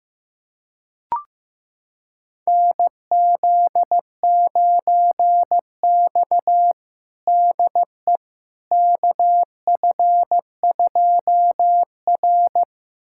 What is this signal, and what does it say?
Morse code, heard as a keyed tone, about 15 wpm, 700 Hz. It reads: NZ9X DE KF2R
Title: USA Callsign DE Callsign - 4 Characters - 15wpm